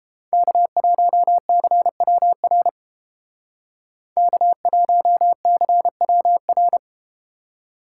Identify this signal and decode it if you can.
Morse code, heard as a keyed tone, about 33 wpm, 700 Hz. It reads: K1CWR K1CWR